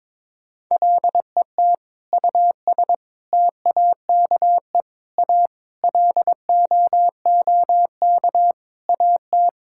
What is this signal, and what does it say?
Morse code, heard as a keyed tone, about 22 wpm, 700 Hz. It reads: LET US TAKE A LOOK AT
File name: practice / 22wpm / Sentences with 8-Word Phrases - 22wpm